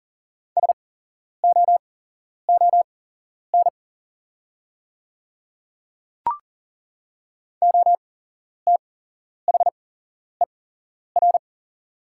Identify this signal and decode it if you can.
Morse code, heard as a keyed tone, about 39 wpm, spaced as for 11 wpm, 690 Hz. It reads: SOON OTHER
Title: 8x Character Spacing - Top 500 Words - 40wpm